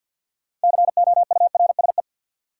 Morse code, heard as a keyed tone, about 54 wpm, 700 Hz. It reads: COURSE
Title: Sets of 2 Words - New Word Drill - 50wpm